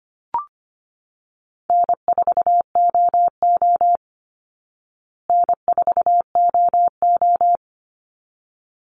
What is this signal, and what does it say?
Morse code, heard as a keyed tone, about 25 wpm, 700 Hz. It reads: N4OO N4OO
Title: Call Signs Repeated 3x - 25wpm